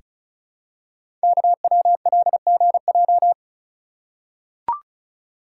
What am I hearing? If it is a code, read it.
Morse code, heard as a keyed tone, about 36 wpm, 700 Hz. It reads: KWLGJ
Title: Code Groups - 5 Letters - 35wpm